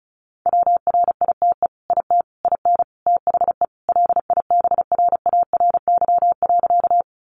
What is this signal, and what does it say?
Morse code, heard as a keyed tone, about 34 wpm, 700 Hz. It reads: WRITE IT IN THE LIBRARY.
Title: Sentences from Top 600 Words 35wpm